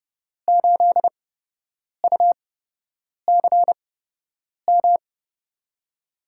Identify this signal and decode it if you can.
Morse code, heard as a keyed tone, about 32 wpm, spaced as for 8 wpm, 700 Hz. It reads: 8UCM
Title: USA Call Signs - 5 Characters - 8x Character Spacing - 30wpm